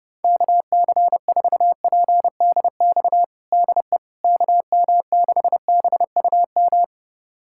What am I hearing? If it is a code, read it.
Morse code, heard as a keyed tone, about 30 wpm, 700 Hz. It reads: KC4PDX DE KM6BUM
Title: USA Callsign DE Callsign 30wpm